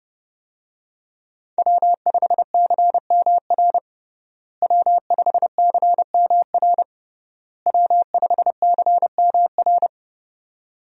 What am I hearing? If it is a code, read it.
Morse code, heard as a keyed tone, about 30 wpm, 700 Hz. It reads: W5CMR W5CMR W5CMR
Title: USA Call Signs - 5 Characters - Repeated 3x - 30wpm